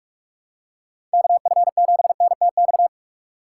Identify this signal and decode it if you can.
Morse code, heard as a keyed tone, about 45 wpm, 700 Hz. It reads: KF7NTX
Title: USA Call Signs - 6 Characters - 3x Character Spacing - 45wpm